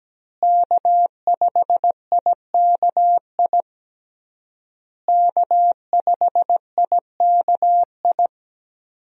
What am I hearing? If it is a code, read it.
Morse code, heard as a keyed tone, about 17 wpm, 700 Hz. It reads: K5IKI K5IKI
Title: USA Call Signs - 5 Characters - Repeated 3x - 17wpm